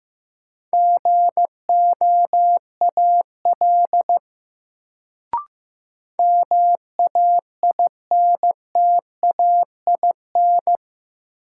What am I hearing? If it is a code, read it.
Morse code, heard as a keyed tone, about 15 wpm, 700 Hz. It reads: GOAL MAINTAIN
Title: Sentences from Top 1000 Words - Review Part 5 - New Word Drill - 15wpm